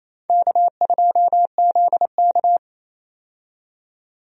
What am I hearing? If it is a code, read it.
Morse code, heard as a keyed tone, about 28 wpm, 700 Hz. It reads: K2ZK
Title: USA Call Signs - 4 Characters - 28wpm